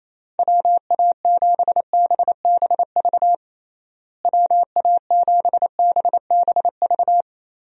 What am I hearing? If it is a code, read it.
Morse code, heard as a keyed tone, about 28 wpm, 700 Hz. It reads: WA7BBV WA7BBV
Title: USA Call Signs - 6 Characters - Repeated 3x - 28wpm